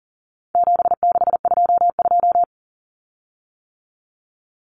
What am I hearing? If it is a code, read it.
Morse code, heard as a keyed tone, about 40 wpm, 700 Hz. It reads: 7622
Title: Common Cut Numbers - 4-Digit 40wpm